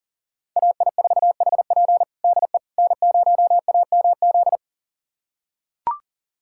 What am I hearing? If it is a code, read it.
Morse code, heard as a keyed tone, about 40 wpm, 700 Hz. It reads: AI4HP DE N0AMZ
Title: USA Callsign DE Callsign - 5 Characters - 40wpm